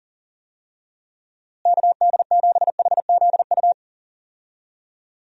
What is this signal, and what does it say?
Morse code, heard as a keyed tone, about 37 wpm, 700 Hz. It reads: KD7HZU
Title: USA Call Signs - 6 Characters - Repeated 3x - 40wpm